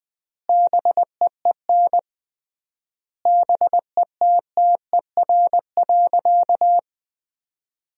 Morse code, heard as a keyed tone, about 20 wpm, 700 Hz. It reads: BEEN BETTER.
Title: 3x Word-Spacing - Sentences from Top 500 Words - 20wpm